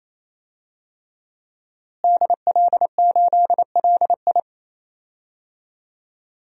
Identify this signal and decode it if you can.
Morse code, heard as a keyed tone, about 27 wpm, 700 Hz. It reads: DL8LI
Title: DXCC-100 Call Signs - 5 Characters - Repeated 3x - 28wpm